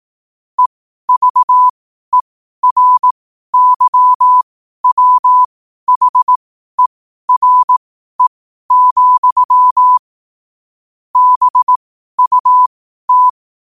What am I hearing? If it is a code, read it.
Morse code, heard as a keyed tone, about 18 wpm, spaced as for 12 wpm, 1000 Hz. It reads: EVERYWHERE, BUT
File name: qotdfiles / 12wpm